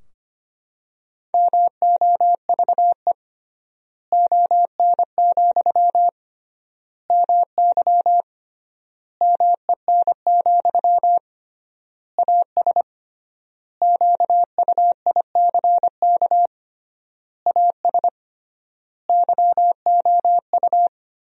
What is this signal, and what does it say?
Morse code, heard as a keyed tone, about 25 wpm, 700 Hz. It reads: MOVE ON, MY MEN, AS QUICK AS YOU
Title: 3x Word-Spacing - Sentences from Top 500 Words - 25wpm